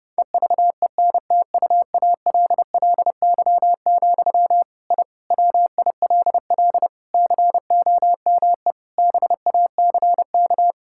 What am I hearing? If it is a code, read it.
Morse code, heard as a keyed tone, about 30 wpm, 700 Hz. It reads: EVENTUALLY, I WILL COME BACK
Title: Sentences from Top 1000 Words - Review Part 6 - 30wpm